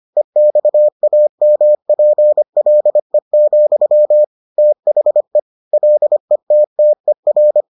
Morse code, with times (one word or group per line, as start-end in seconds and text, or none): 0.2-4.2 EXAMPLE,
4.6-5.4 THE
5.7-7.6 LETTER